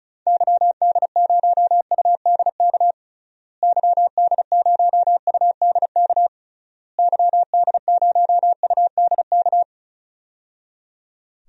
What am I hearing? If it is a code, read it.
Morse code, heard as a keyed tone, about 36 wpm, 700 Hz. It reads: YD0UDK YD0UDK YD0UDK
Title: DXCC-100 Call Signs - 6 Characters - Repeated 3x - 35wpm